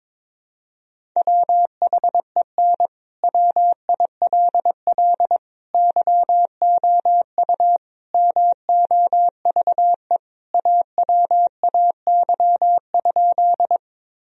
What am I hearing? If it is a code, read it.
Morse code, heard as a keyed tone, about 23 wpm, 700 Hz. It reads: WHEN WILL YOU MOVE AWAY?
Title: Sentences from Top 1000 Words - Review Part 1 - 22wpm